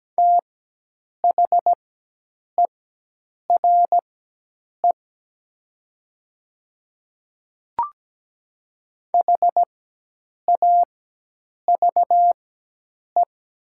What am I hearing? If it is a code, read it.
Morse code, heard as a keyed tone, about 17 wpm, spaced as for 8 wpm, 700 Hz. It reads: THERE HAVE
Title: 4x Character Spacing - Top 100 Words - 17wpm